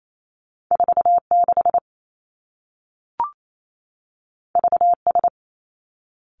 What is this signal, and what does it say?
Morse code, heard as a keyed tone, about 28 wpm, 700 Hz. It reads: 4 6 V S
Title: Two Challenging Letter-Number Combinations 28wpm